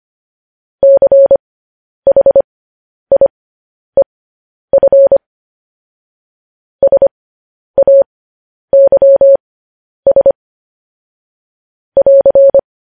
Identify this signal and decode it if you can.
Morse code, heard as a keyed tone, about 25 wpm, spaced as for 10 wpm, 550 Hz. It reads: CHIEF SAYS <AR>
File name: podcast / News headlines-1x-25x10